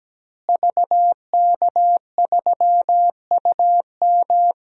Morse code, heard as a keyed tone, about 17 wpm, 700 Hz. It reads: VK3UM
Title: DXCC-100 Call Signs - 5 Characters - Repeated 3x - 17wpm